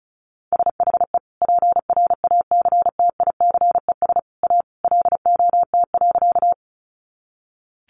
Morse code, heard as a keyed tone, about 34 wpm, 700 Hz. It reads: SHE PRACTICES A LOT.
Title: Additional Easy Phrases 35wpm